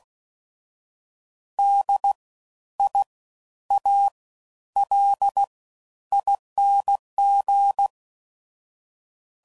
Morse code, heard as a keyed tone, about 16 wpm, spaced as for 10 wpm, 800 Hz. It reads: DIALING
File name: Suffixes / Special INGs / Fast Ings